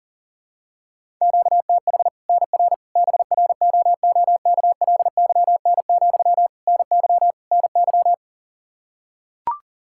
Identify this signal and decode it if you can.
Morse code, heard as a keyed tone, about 40 wpm, 700 Hz. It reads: QTH NR BROOKLYN, NY NY
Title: Practice QSO - Exchange 1 - Beginning - 40wpm